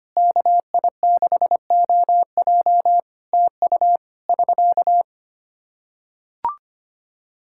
Morse code, heard as a keyed tone, about 25 wpm, 700 Hz. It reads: KI6OJ TU <SK>
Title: Practice QSO - Exchange 4 - Ending - 25wpm